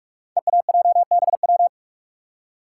Morse code, heard as a keyed tone, about 44 wpm, 690 Hz. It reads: EAJBW
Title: Code Groups - 5 Letters - 45wpm